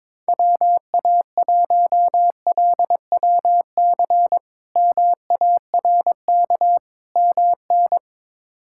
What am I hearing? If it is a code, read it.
Morse code, heard as a keyed tone, about 22 wpm, 700 Hz. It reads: WA1LWC MARK MN
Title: K1USN SST QSO Practice - 22wpm